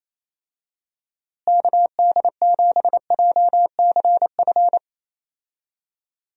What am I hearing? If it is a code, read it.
Morse code, heard as a keyed tone, about 28 wpm, 700 Hz. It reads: KD7JCF
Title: USA Call Signs - 6 Characters - Repeated 2x - 28wpm